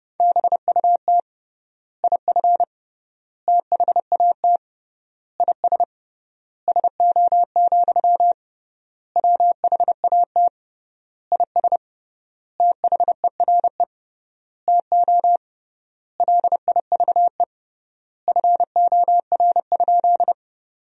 Morse code, heard as a keyed tone, about 32 wpm, 700 Hz. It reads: BUT IF THAT IS SO, WHAT IS THERE TO LIVE FOR?
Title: 3x Word-Spacing - Sentences from Top 200 Words - 30wpm